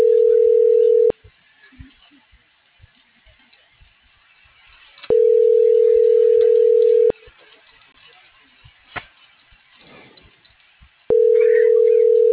Ringback tone (North America), 440 + 480 Hz.